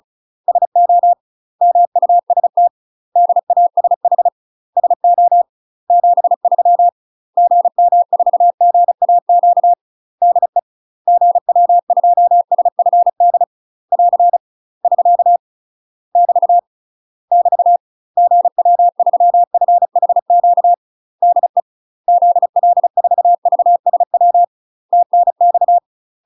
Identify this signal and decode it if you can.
Morse code, heard as a keyed tone, about 35 wpm, 700 Hz. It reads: SO MUST DASH SO 73 GM4GAQ DE GW2SFD <AR> <SK> <BT> <BT> GW3FHQ DE ZL4VSW TNX
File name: UK MASTER QSOs TRAINING / 35WPM / UK MASTER QSOs 0016